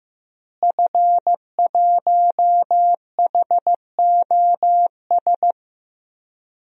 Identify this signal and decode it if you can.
Morse code, heard as a keyed tone, about 15 wpm, 700 Hz. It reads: F1HOS
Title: DXCC-100 Call Signs - 5 Characters - Repeated 2x - 15wpm